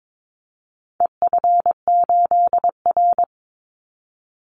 Morse code, heard as a keyed tone, about 22 wpm, 700 Hz. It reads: EF8R
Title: Call Signs Repeated 2x - 22wpm